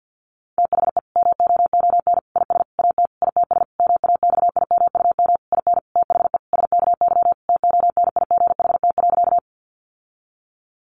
Morse code, heard as a keyed tone, about 50 wpm, 700 Hz. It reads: THE MOON IS AT ITS MAXIMUM IN THE SKY TONIGHT.
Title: Sentences from Top 1300 Words 50wpm